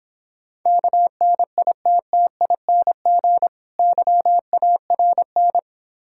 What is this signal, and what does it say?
Morse code, heard as a keyed tone, about 26 wpm, 700 Hz. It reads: KNITTING YARN